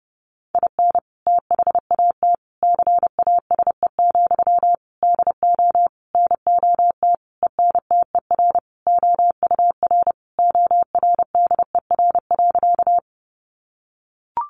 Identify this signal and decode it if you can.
Morse code, heard as a keyed tone, about 30 wpm, 700 Hz. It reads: IN THAT CASE, DO NOT ENTER OUR ORDER.
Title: Sentences from Top 1300 Words 30wpm